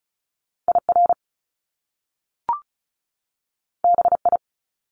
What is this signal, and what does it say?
Morse code, heard as a keyed tone, about 36 wpm, 700 Hz. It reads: IR BI
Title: Top Two Letter Combinations in Words 35wpm